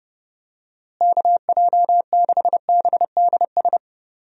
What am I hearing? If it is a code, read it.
Morse code, heard as a keyed tone, about 30 wpm, 700 Hz. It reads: KJ6BDS
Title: USA Call Signs - 6 Characters - Repeated 2x - 30wpm